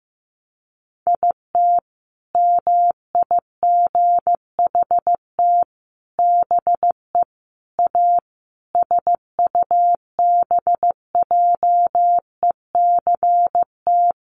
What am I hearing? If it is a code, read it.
Morse code, heard as a keyed tone, about 15 wpm, 700 Hz. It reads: IT MIGHT BE A SUBJECT
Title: Sentences from Top 1500 Words 15wpm